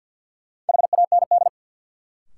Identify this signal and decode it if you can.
Morse code, heard as a keyed tone, about 53 wpm, 700 Hz. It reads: HAND
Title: Sets of 3 Words - New Word Drill - 50wpm